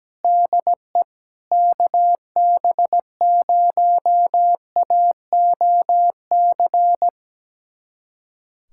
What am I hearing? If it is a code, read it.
Morse code, heard as a keyed tone, about 17 wpm, 700 Hz. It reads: DE KB0AOC
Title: USA Callsign DE Callsign - 6 Characters - 17wpm